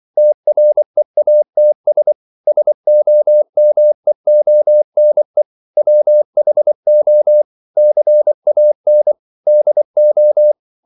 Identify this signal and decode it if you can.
Morse code, heard as a keyed tone, about 24 wpm, 600 Hz. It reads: TREATS SOMEONE WHO CAN DO
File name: CW / Practice / EnglishQuotations0224